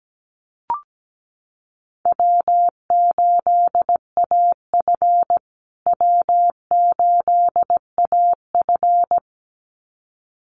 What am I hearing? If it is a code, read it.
Morse code, heard as a keyed tone, about 17 wpm, 700 Hz. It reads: W8AF W8AF
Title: USA Call Signs Repeated 2x 17wpm